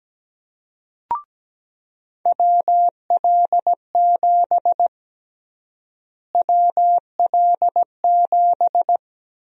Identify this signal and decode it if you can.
Morse code, heard as a keyed tone, about 17 wpm, 700 Hz. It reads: WL7 WL7
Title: USA Call Sign Prefixes - Repeated 2x - 17wpm